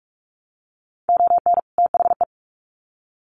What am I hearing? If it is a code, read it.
Morse code, heard as a keyed tone, about 44 wpm, 700 Hz. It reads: ON THE
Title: Sets of 2 Words 45wpm